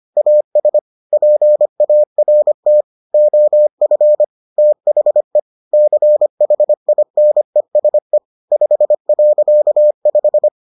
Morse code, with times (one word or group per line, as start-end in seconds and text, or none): 0.2-0.8 AS
1.1-2.8 PART
3.1-4.2 OF
4.6-5.4 THE
5.7-8.2 CHINESE
8.5-10.5 5.5